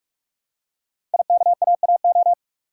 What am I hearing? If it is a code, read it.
Morse code, heard as a keyed tone, about 44 wpm, 700 Hz. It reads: IKAAO
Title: Code Groups - 5 Letters - 45wpm